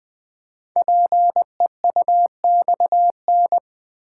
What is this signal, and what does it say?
Morse code, heard as a keyed tone, about 20 wpm, 700 Hz. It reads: PEUXN
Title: Code Groups - 5 Letters - 20wpm